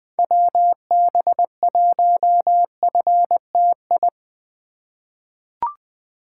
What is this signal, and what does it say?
Morse code, heard as a keyed tone, about 20 wpm, 700 Hz. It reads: WB1FTI
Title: USA Call Signs - 6 Characters - 4x Character Spacing - 20wpm